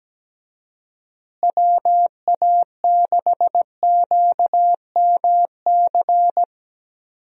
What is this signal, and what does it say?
Morse code, heard as a keyed tone, about 17 wpm, 700 Hz. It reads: WA6QMC
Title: USA Call Signs - 6 Characters - Repeated 2x - 17wpm